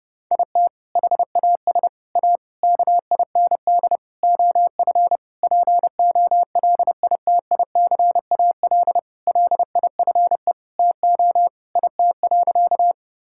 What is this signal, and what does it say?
Morse code, heard as a keyed tone, about 30 wpm, 700 Hz. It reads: IT HAS A KIND OF POLITICAL LIFE TO IT.
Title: Sentences from Top 1000 Words - Review Part 5 - 30wpm